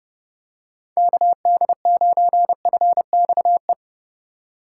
Morse code, heard as a keyed tone, about 30 wpm, 700 Hz. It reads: KD9FXE
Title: USA Call Signs - 6 Characters - 3x Character Spacing - 30wpm